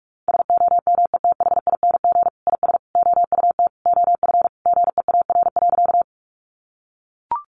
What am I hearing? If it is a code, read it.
Morse code, heard as a keyed tone, about 44 wpm, 700 Hz. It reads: SOMETHING IS OUT OF GEAR.
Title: Sentences from Top 1400 Words 45wpm